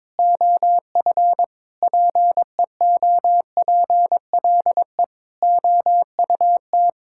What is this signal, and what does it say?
Morse code, heard as a keyed tone, about 22 wpm, 700 Hz. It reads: OF PEOPLE OUT
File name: Sentences with 9-Word Phrases - 22wpm